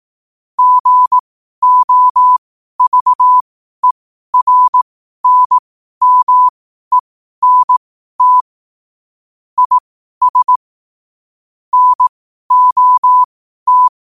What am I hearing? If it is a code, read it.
Morse code, heard as a keyed tone, about 18 wpm, spaced as for 12 wpm, 1000 Hz. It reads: GOVERNMENT IS NOT
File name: qotdfiles / 12wpm